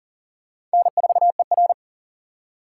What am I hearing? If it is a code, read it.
Morse code, heard as a keyed tone, about 40 wpm, 700 Hz. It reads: N4ER
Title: USA Call Signs - 4 Characters - 3x Character Spacing - 40wpm